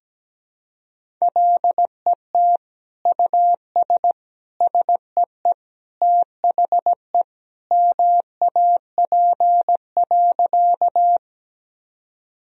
Morse code, heard as a keyed tone, about 17 wpm, 700 Hz. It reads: LET US SEE THE MAP.
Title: Sentences from Top 1000 Words - Review Part 2 - 17wpm